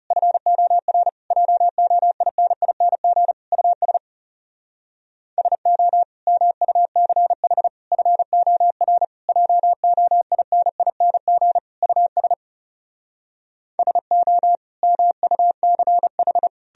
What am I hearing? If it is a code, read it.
Morse code, heard as a keyed tone, about 40 wpm, 700 Hz. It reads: FOR JOINING US SO MUCH FOR JOINING US SO MUCH